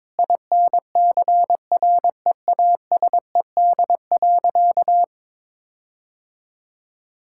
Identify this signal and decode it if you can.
Morse code, heard as a keyed tone, about 22 wpm, 700 Hz. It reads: INCREASED.
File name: practice / 22wpm / Sentences with 8-Word Phrases - 22wpm